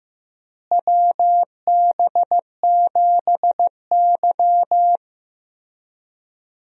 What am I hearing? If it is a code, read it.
Morse code, heard as a keyed tone, about 16 wpm, 700 Hz. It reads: WB7Y
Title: USA Call Signs - 4 Characters - Repeated 2x - 15wpm